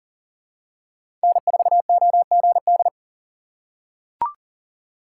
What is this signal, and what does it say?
Morse code, heard as a keyed tone, about 40 wpm, 700 Hz. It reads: N4OGD
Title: USA Call Signs - 5 Characters - 3x Character Spacing - 40wpm